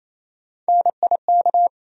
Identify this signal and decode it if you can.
Morse code, heard as a keyed tone, about 28 wpm, 700 Hz. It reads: NIK
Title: USA Call Sign Postfixes - 3 Letters 28wpm